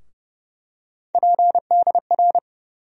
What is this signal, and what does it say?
Morse code, heard as a keyed tone, about 32 wpm, 700 Hz. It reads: PDR
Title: USA Call Sign Postfixes - 3 Letters 30wpm